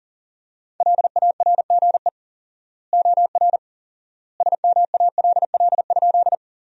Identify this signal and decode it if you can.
Morse code, heard as a keyed tone, about 40 wpm, 700 Hz. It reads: LARGE OR SMALL?
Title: 4x Word-Spacing - Sentences from Top 200 Words - 40wpm